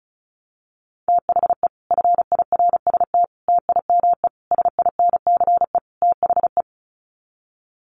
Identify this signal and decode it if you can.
Morse code, heard as a keyed tone, about 34 wpm, 700 Hz. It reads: THE FIRST TIME SINCE THE
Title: Sets of 5 Words Encore 35wpm